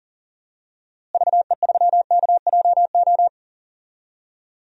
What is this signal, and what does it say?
Morse code, heard as a keyed tone, about 39 wpm, 700 Hz. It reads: VE3KJO
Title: DXCC-100 Call Signs - 6 Characters - Repeated 2x - 40wpm